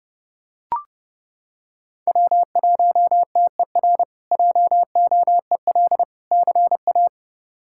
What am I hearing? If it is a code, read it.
Morse code, heard as a keyed tone, about 30 wpm, 700 Hz. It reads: W1TER JOEL CA
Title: K1USN SST QSO Practice - 30wpm